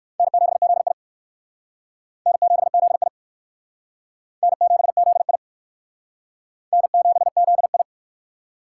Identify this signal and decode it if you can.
Morse code, heard as a keyed tone, about 58 wpm, 700 Hz. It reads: N7ZI N7ZI N7ZI N7ZI